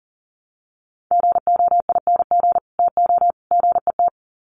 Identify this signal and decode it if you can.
Morse code, heard as a keyed tone, about 40 wpm, 700 Hz. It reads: GOING TO GET
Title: Sets of 3 Words 40wpm